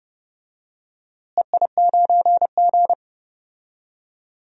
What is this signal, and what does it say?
Morse code, heard as a keyed tone, about 31 wpm, 700 Hz. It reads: EI9G
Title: DXCC-100 Call Signs - 4 Characters - Repeated 2x - 30wpm